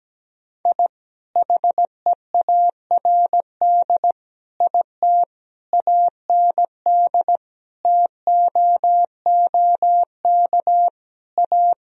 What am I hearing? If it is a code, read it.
Morse code, heard as a keyed tone, about 17 wpm, 700 Hz. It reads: I HEARD IT AND TOOK A
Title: Sentences from Top 1000 Words - Review Part 1 - 17wpm